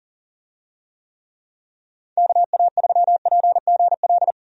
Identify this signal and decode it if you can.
Morse code, heard as a keyed tone, about 40 wpm, 700 Hz. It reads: KA3PGL
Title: USA Call Signs - 6 Characters - 40wpm